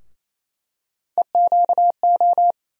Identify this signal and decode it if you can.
Morse code, heard as a keyed tone, about 28 wpm, 700 Hz. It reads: EQO
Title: USA Call Sign Postfixes - 3 Letters 28wpm